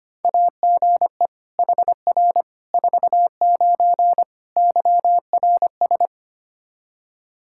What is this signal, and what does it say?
Morse code, heard as a keyed tone, about 25 wpm, 700 Hz. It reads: AGE HR 49 YRS
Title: K1USN SST QSO Practice 25wpm